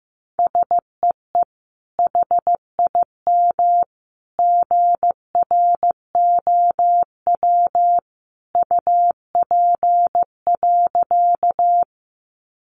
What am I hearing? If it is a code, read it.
Morse code, heard as a keyed tone, about 15 wpm, 700 Hz. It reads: SEE HIM GROW UP.
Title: Sentences from Top 300 Words 15wpm